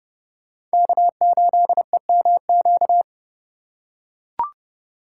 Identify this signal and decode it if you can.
Morse code, heard as a keyed tone, about 30 wpm, 700 Hz. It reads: K8EMQ